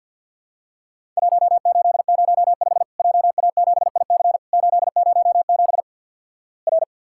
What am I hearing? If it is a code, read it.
Morse code, heard as a keyed tone, about 50 wpm, 700 Hz. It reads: 1805 JA7IC 807 R
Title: Worked All Europe DX Contest - QTC Exchange - 50wpm